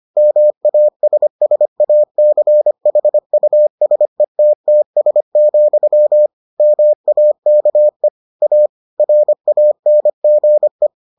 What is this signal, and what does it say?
Morse code, heard as a keyed tone, about 25 wpm, 600 Hz. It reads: MASSACHUSETTS, MAKE A RANGE